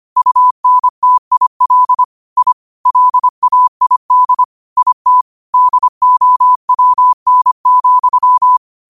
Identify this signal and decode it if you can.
Morse code, heard as a keyed tone, about 25 wpm, 1000 Hz. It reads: UNTIL I LAID IT DOWN,
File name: qotdfiles / 25wpm